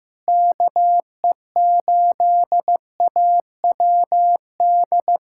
Morse code, heard as a keyed tone, about 15 wpm, 700 Hz. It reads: KE8AWD
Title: USA Callsign DE Callsign - 6 Characters - 15wpm